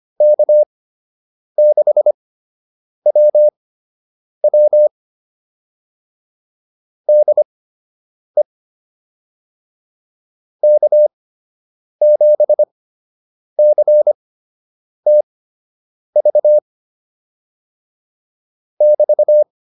Morse code, heard as a keyed tone, about 25 wpm, spaced as for 8 wpm, 600 Hz. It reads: K6WW DE K7CTV <BT>